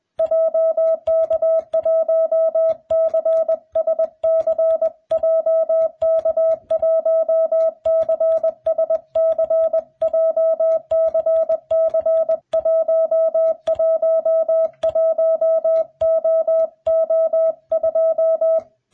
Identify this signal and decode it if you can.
Morse code, heard as a keyed tone, about 21 wpm, 660 Hz. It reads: JK1CSC JK1CSC JCC111OO2